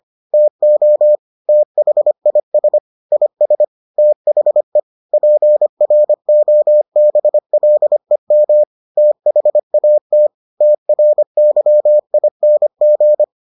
Morse code, heard as a keyed tone, about 25 wpm, 600 Hz. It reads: TO THIS IS THE PROBLEM THAT TRYING